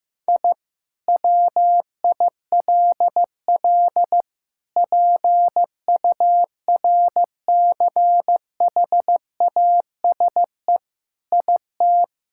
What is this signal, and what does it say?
Morse code, heard as a keyed tone, about 15 wpm, 700 Hz. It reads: I WILL PURCHASE IT
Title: Sentences from Top 1000 Words - Review Part 5 - 15wpm